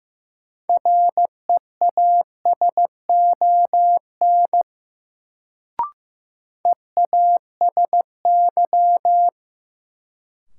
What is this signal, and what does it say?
Morse code, heard as a keyed tone, about 15 wpm, 700 Hz. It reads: REASON EASY
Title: Sentences from Top 700 Words - New Word Drill - 15wpm